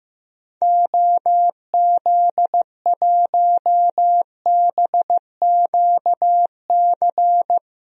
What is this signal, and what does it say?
Morse code, heard as a keyed tone, about 15 wpm, 700 Hz. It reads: OZ1BQC
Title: DXCC-100 Call Signs - 6 Characters - Repeated 2x - 15wpm